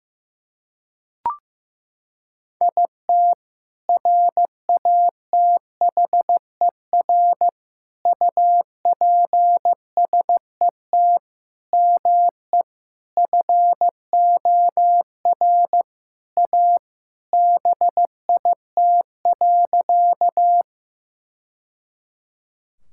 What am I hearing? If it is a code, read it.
Morse code, heard as a keyed tone, about 15 wpm, 700 Hz. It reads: IT RATHER UPSET ME FOR A BIT.
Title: Sentences from Top 1000 Words - Review Part 5 - 15wpm